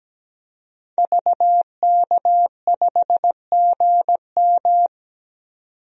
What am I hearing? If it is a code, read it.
Morse code, heard as a keyed tone, about 17 wpm, 700 Hz. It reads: VK5GM
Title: DXCC-100 Call Signs - 5 Characters - 17wpm